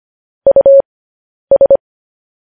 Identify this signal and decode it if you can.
Morse code, heard as a keyed tone, about 25 wpm, spaced as for 10 wpm, 550 Hz. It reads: US
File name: podcast / News headlines-1x-25x10